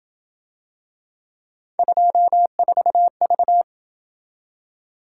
Morse code, heard as a keyed tone, about 26 wpm, 700 Hz. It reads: 24V